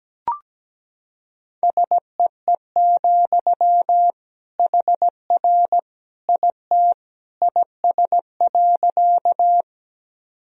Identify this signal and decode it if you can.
Morse code, heard as a keyed tone, about 17 wpm, 700 Hz. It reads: SEE, HR IT IS.
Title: Sentences from Top 200 Words and Abbreviations 17wpm